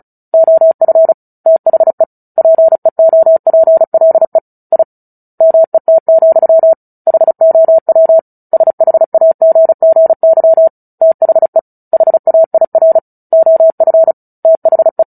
Morse code, heard as a keyed tone, about 34 wpm, 650 Hz. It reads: OF THE PEOPLE I MET, HOW SHAGGY THE HAIR OF THE
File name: Book Two / 09-Chapter 9 - 35 WPM